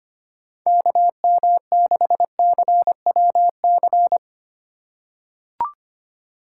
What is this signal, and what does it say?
Morse code, heard as a keyed tone, about 25 wpm, 700 Hz. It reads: KM6CWC
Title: USA Call Signs - 6 Characters - 4x Character Spacing - 25wpm